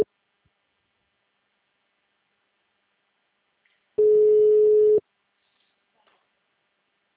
Ringback tone (Europe, 425 Hz).
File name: out- -2016- - - .201